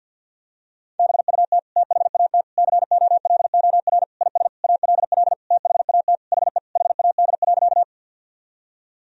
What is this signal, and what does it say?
Morse code, heard as a keyed tone, about 49 wpm, 700 Hz. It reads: BUT THAT COLOR IS ALL THAT HE HAD.
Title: Sentences from Top 1000 Words - Review Part 2 - 50wpm